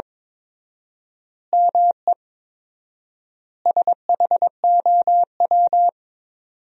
Morse code, heard as a keyed tone, about 22 wpm, 700 Hz. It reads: ME SHOW
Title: 4x Word-Spacing - Sentences from Top 300 Words - 22wpm